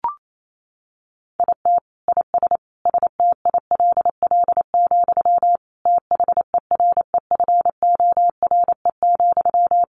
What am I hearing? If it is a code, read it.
Morse code, heard as a keyed tone, about 28 wpm, 700 Hz. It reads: IT IS STILL, THEREFORE,
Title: Sentences from Top 1000 Words 28wpm